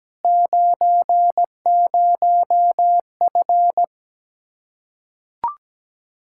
Morse code, heard as a keyed tone, about 17 wpm, 700 Hz. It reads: 90F
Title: Practice QSO - Exchange 2 - Info - 17wpm